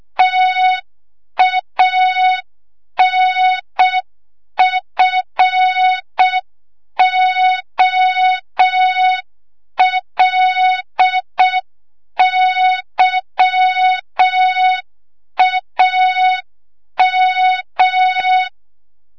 Morse code, heard as a keyed tone, about 6 wpm, 3700 Hz. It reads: TANFOLYAM
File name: Morze-HA2MN / hang / 25 GYAKORLAT